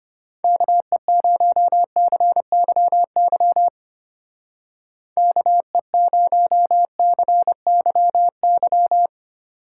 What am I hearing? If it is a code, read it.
Morse code, heard as a keyed tone, about 33 wpm, 700 Hz. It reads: KE0CYY KE0CYY